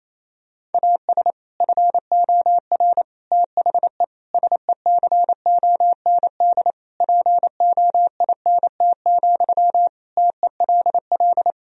Morse code, heard as a keyed tone, about 28 wpm, 700 Hz. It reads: AS FOR THE SECOND POINT, TELL
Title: Sentences from Top 1000 Words - Review Part 1 - 28wpm